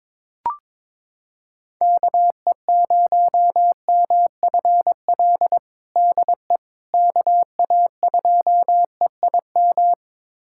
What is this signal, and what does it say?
Morse code, heard as a keyed tone, about 21 wpm, 700 Hz. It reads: KE0MFL DE KA2EIM
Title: USA Callsign DE Callsign - 6 Characters - 22wpm